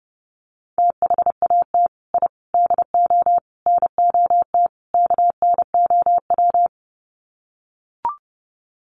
Morse code, heard as a keyed tone, about 30 wpm, 700 Hz. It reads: THAT I DO NOT KNOW
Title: Sets of 5 Words Encore2 30wpm